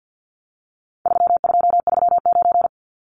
Morse code, heard as a keyed tone, about 50 wpm, 700 Hz. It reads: 3239
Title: Numbers - 4-Digit 50wpm